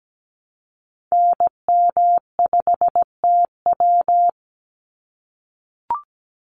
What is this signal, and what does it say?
Morse code, heard as a keyed tone, about 17 wpm, 700 Hz. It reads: NM5TW
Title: NA SOTA Activator Callsigns 17wpm